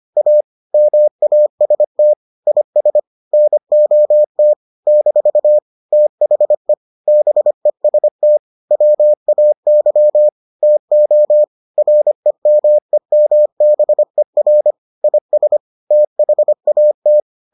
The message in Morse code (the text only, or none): A MAST IS NOT - THE BEST WAY TO REMEMBER IS THAT